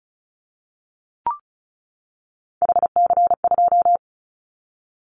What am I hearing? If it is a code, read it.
Morse code, heard as a keyed tone, about 34 wpm, 700 Hz. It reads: HC2
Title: Callsign Prefixes for Top 100 DXCC - 35wpm